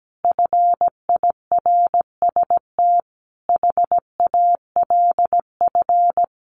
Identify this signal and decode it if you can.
Morse code, heard as a keyed tone, about 17 wpm, 700 Hz. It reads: FIRST HALF
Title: Sets of 5 Words 17wpm